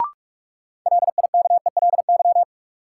Morse code, heard as a keyed tone, about 44 wpm, 700 Hz. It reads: LIKELY
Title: Sets of 2 Words - New Word Drill - 45wpm